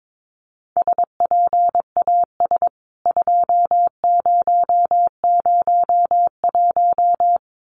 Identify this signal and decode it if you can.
Morse code, heard as a keyed tone, about 22 wpm, 700 Hz. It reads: SPAS 2001
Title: CWT Exchange 22wpm